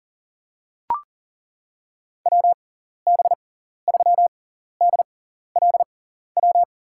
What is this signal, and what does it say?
Morse code, heard as a keyed tone, about 40 wpm, spaced as for 14 wpm, 700 Hz. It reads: WB3DLW
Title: USA Call Signs - 6 Characters - 6x Character Spacing - 40wpm